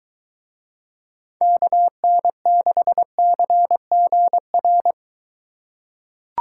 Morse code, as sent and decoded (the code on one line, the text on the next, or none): -.- -. -.... -.-. --. .-.
KN6CGR